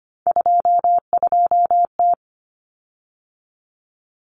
Morse code, heard as a keyed tone, about 25 wpm, 700 Hz. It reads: 22T
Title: Common Cut Numbers - 3-Digit 25wpm